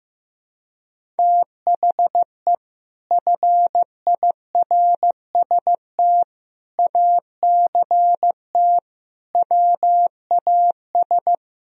Morse code, heard as a keyed tone, about 15 wpm, 700 Hz. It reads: THE FIRST ACT WAS
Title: Sentences from Top 1000 Words - Review Part 1 - 15wpm